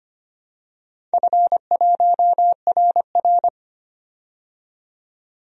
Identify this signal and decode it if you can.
Morse code, heard as a keyed tone, about 25 wpm, 700 Hz. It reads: F1RR
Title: DXCC-100 Call Signs - 4 Characters - 25wpm